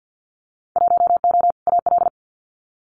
Morse code, heard as a keyed tone, about 50 wpm, 700 Hz. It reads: 1O AL
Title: Winter Field Day Exchanges 50wpm